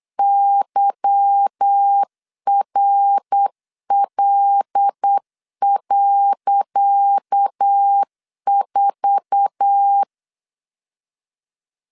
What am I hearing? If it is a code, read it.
Morse code, heard as a keyed tone, about 8 wpm, 800 Hz. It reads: YRL.4
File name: LEZIONI / 10 SIGNS / 10C DOT 8wpm 3649